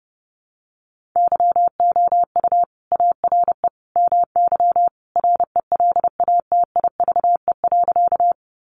Morse code, heard as a keyed tone, about 30 wpm, 700 Hz. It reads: YOU ARE MY RELATIVE.
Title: Sentences from Top 1400 Words 30wpm